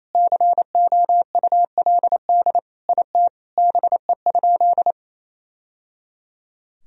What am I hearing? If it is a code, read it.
Morse code, heard as a keyed tone, about 28 wpm, 700 Hz. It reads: COULD IT BE?
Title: Sentences from Top 1000 Words - Review Part 2 - 28wpm